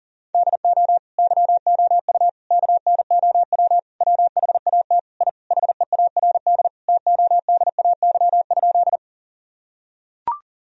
Morse code, heard as a keyed tone, about 40 wpm, 700 Hz. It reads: DO YOU KNOW WHAT I HEARD TODAY?
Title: Sentences from Top 1000 Words - Review Part 4 - 40wpm